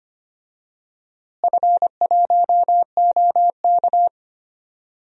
Morse code, heard as a keyed tone, about 25 wpm, 700 Hz. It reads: F1OK
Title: DXCC-100 Call Signs - 4 Characters - Repeated 3x - 25wpm